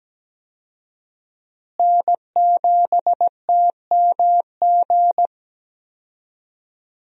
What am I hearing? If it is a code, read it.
Morse code, heard as a keyed tone, about 17 wpm, 700 Hz. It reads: N7TMG
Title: USA Call Signs - 5 Characters - Repeated 3x - 17wpm